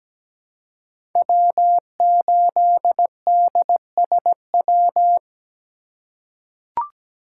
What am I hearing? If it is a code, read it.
Morse code, heard as a keyed tone, about 17 wpm, 700 Hz. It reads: W8DSW
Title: USA Call Signs - 4x Character Spacing - 17wpm